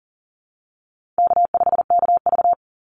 Morse code, heard as a keyed tone, about 40 wpm, 700 Hz. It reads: K5KV
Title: Call Signs Repeated 2x - 40wpm